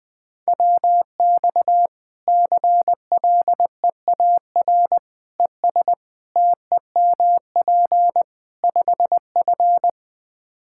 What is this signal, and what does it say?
Morse code, heard as a keyed tone, about 21 wpm, 700 Hz. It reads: WX CLEAR ES TEMP 5F
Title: Practice QSO - Exchange 2 - Info - 20wpm